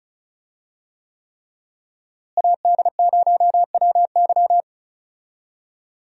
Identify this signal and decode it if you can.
Morse code, heard as a keyed tone, about 34 wpm, 700 Hz. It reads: AD0WY
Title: USA Call Signs - 5 Characters - Repeated 3x - 35wpm